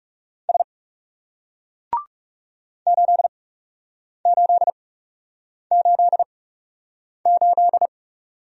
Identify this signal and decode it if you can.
Morse code, heard as a keyed tone, about 39 wpm, spaced as for 8 wpm, 700 Hz. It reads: S 8888